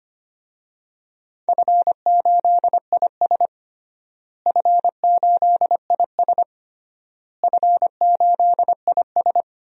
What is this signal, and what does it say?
Morse code, heard as a keyed tone, about 25 wpm, 700 Hz. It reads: F8IS F8IS F8IS
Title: DXCC-100 Call Signs - 4 Characters - Repeated 3x - 25wpm